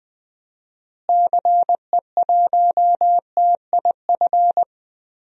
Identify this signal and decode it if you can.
Morse code, heard as a keyed tone, about 20 wpm, 700 Hz. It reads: CE1TIF
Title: DXCC-100 Call Signs - 6 Characters - Repeated 2x - 20wpm